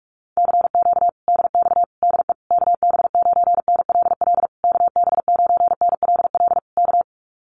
Morse code, heard as a keyed tone, about 44 wpm, 700 Hz. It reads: CQ DX DE KB9NLL KB9NLL K
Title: Common QSO Phrases Encore 45wpm